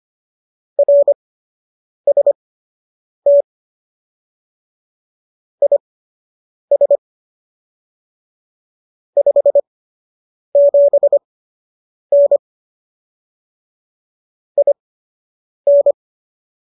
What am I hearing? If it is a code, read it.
Morse code, heard as a keyed tone, about 25 wpm, spaced as for 8 wpm, 570 Hz. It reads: RST IS 57N IN